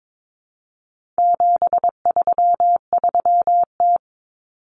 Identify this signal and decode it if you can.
Morse code, heard as a keyed tone, about 22 wpm, 700 Hz. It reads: 733T
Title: Common Cut Numbers - 4-Digit 22wpm